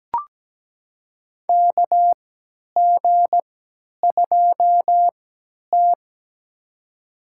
Morse code, heard as a keyed tone, about 17 wpm, spaced as for 10 wpm, 700 Hz. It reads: KG2T
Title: USA Call Signs - 4 Characters - 3x Character Spacing - 17wpm